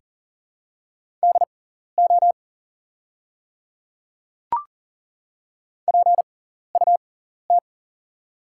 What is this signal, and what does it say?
Morse code, heard as a keyed tone, about 40 wpm, spaced as for 14 wpm, 700 Hz. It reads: DO PUT
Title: 6x Character Spacing - Top 200 Words - 40wpm